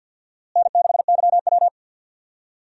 Morse code, heard as a keyed tone, about 49 wpm, 700 Hz. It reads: N6YW
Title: USA Call Signs - 4 Characters - Repeated 2x - 50wpm